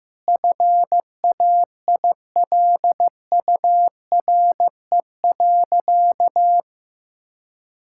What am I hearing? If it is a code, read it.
Morse code, heard as a keyed tone, about 16 wpm, 700 Hz. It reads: FAILURE.
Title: Sentences from Top 1000 Words - Review Part 6 - 15wpm